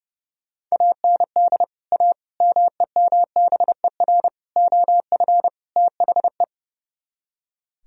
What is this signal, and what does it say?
Morse code, heard as a keyed tone, about 37 wpm, 700 Hz. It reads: AND A MEMBER OF THE
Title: Speed-Racing - Sets of 5 Words 30wpm